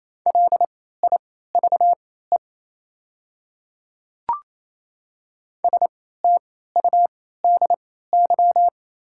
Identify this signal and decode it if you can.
Morse code, heard as a keyed tone, about 28 wpm, spaced as for 16 wpm, 690 Hz. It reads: LIVE STUDY
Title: 3x Character Spacing - Top 500 Words - 28wpm